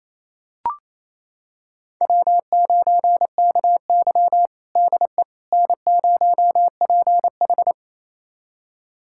Morse code, heard as a keyed tone, about 28 wpm, 700 Hz. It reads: W9KY DE N0PH
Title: USA Callsign DE Callsign - 4 Characters - 28wpm